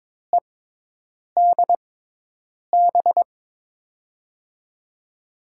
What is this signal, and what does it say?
Morse code, heard as a keyed tone, about 21 wpm, spaced as for 8 wpm, 700 Hz. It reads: EDB
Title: USA Call Signs - 5 Characters - 6x Character Spacing - 22wpm